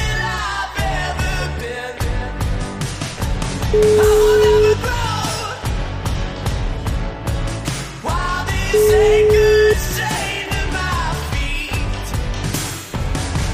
Ringback tone (Europe, 425 Hz), with noise.